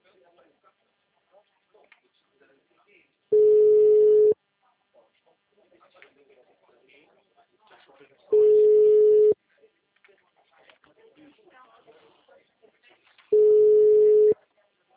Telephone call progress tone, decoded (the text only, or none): ringback tone